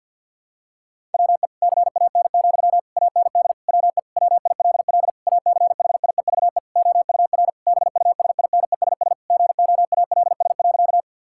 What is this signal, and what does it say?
Morse code, heard as a keyed tone, about 50 wpm, 700 Hz. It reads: WE CAN, AND WE WILL ACHIEVE OUR BUSINESS GOALS.
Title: Sentences from Top 1000 Words - Review Part 6 - 50wpm